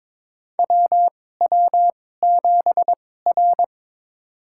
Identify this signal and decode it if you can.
Morse code, heard as a keyed tone, about 22 wpm, spaced as for 16 wpm, 700 Hz. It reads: WW7R
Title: USA Call Signs - 4 Characters - 2x Character Spacing - 22wpm